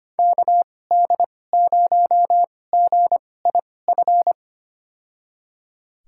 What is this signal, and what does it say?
Morse code, heard as a keyed tone, about 25 wpm, spaced as for 18 wpm, 700 Hz. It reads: KD0GIF
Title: USA Call Signs - 6 Characters - 2x Character Spacing - 25wpm